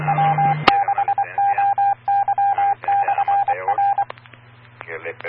Morse code, heard as a keyed tone, about 24 wpm, 800 Hz. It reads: W6OY/R
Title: Claremont, CA Police, Amateur Radio